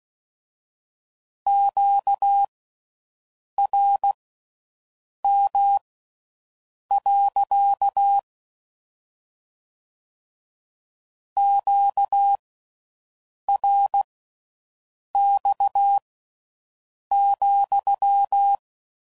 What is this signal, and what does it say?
Morse code, heard as a keyed tone, about 16 wpm, spaced as for 6 wpm, 800 Hz. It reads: QRM. QRX,